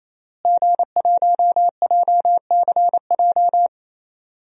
Morse code, heard as a keyed tone, about 28 wpm, 700 Hz. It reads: G1JCJ
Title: DXCC-100 Call Signs - 5 Characters - Repeated 3x - 28wpm